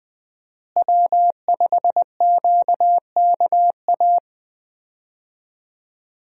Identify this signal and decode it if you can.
Morse code, heard as a keyed tone, about 20 wpm, 700 Hz. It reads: W5QKA
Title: USA Call Signs - 5 Characters - Repeated 2x - 20wpm